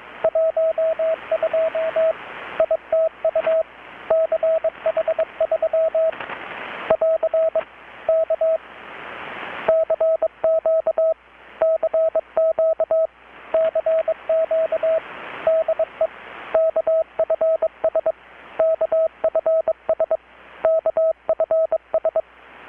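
Morse code, heard as a keyed tone, about 21 wpm, 650 Hz, with radio noise. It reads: 12 ITU CH3 <AR> K CQ CQ CQ DE KFS KFS KFS
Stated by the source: Morse from KPH's Night of Nights XX maritime radiotelegraphy event, received in Chiapas, Mexico about 3,800 km away on a Kenwood R-600 with a grounded 10 m long-wire antenna.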